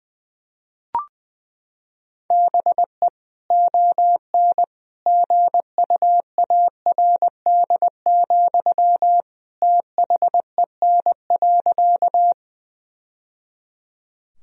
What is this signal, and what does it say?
Morse code, heard as a keyed tone, about 20 wpm, 700 Hz. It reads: BE ON GUARD, THEN.
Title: Sentences from Top 1000 Words - Review Part 4 - 20wpm